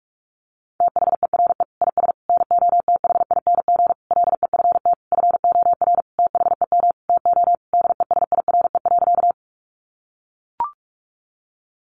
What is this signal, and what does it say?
Morse code, heard as a keyed tone, about 44 wpm, 700 Hz. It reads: THERE IS NOTHING LEFT FOR THEM TO DESIRE.
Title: Sentences from Top 1100 Words 45wpm